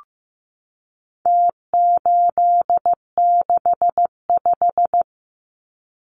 Morse code, heard as a keyed tone, about 15 wpm, 700 Hz. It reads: T865
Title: Numbers - Contest Style - 4-Digit 15wpm